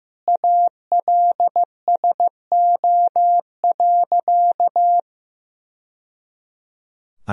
Morse code, heard as a keyed tone, about 15 wpm, 700 Hz. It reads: ALSO.
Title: Sentences from Top 1000 Words - Review Part 1 - 15wpm